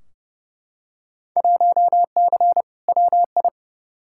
Morse code, heard as a keyed tone, about 31 wpm, 700 Hz. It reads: 1C WI
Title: Field Day Exchanges - 30wpm